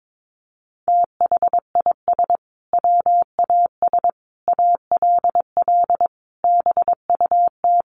Morse code, heard as a keyed tone, about 22 wpm, 700 Hz. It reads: THIS WAS ALL BUT
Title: Sentences from Top 1400 Words 22wpm